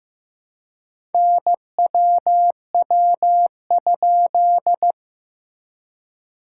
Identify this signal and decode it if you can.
Morse code, heard as a keyed tone, about 15 wpm, 700 Hz. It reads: NWW?
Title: SOTA QSO Practice 15wpm